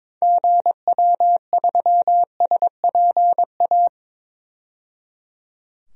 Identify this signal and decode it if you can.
Morse code, heard as a keyed tone, about 22 wpm, 700 Hz. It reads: GW3SPA
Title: DXCC-100 Call Signs - 6 Characters - 22wpm